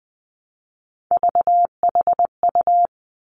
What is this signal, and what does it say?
Morse code, heard as a keyed tone, about 20 wpm, 700 Hz. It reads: VHU
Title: Three Challenging Letter-Number Combinations 20wpm